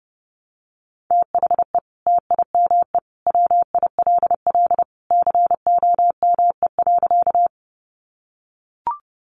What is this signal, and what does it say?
Morse code, heard as a keyed tone, about 30 wpm, 700 Hz. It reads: THE TIME WILL COME.
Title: Sentences from Top 100 Words 30wpm